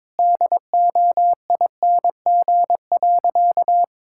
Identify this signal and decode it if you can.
Morse code, heard as a keyed tone, about 22 wpm, 700 Hz. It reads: DOING.
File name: practice / 22wpm / Sentences with 4-Word Phrases - 22wpm